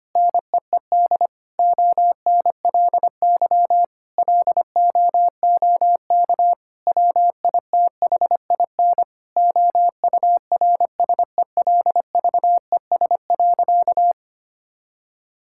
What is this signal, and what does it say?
Morse code, heard as a keyed tone, about 25 wpm, 700 Hz. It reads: NEED ONLY LOOK WITHIN OURSELVES.
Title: Sentences from Top 1600 Words 25wpm